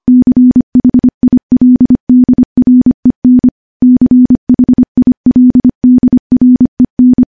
Morse code, heard as a keyed tone, about 25 wpm, 250 Hz. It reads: CHILDREN CHILDREN
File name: morse / morse-training-25WPM